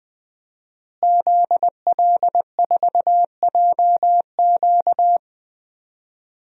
Morse code, heard as a keyed tone, about 20 wpm, 700 Hz. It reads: ZL4JQ
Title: DXCC-100 Call Signs - 5 Characters - Repeated 3x - 20wpm